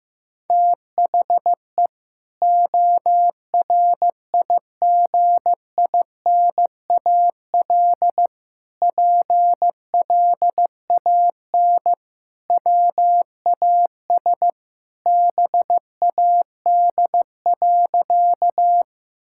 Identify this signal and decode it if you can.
Morse code, heard as a keyed tone, about 16 wpm, 700 Hz. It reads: THE ORIGINAL PLAN WAS BAD.
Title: Sentences from Top 1000 Words - Review Part 6 - 15wpm